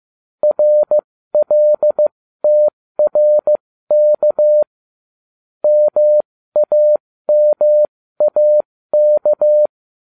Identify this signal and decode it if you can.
Morse code, heard as a keyed tone, about 15 wpm, spaced as for 13 wpm, 600 Hz. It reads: RLTRK MAMAK